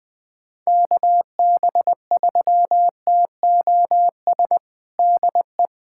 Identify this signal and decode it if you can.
Morse code, heard as a keyed tone, about 20 wpm, 700 Hz. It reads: KB3TOS DE
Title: USA Callsign DE Callsign - 6 Characters - 20wpm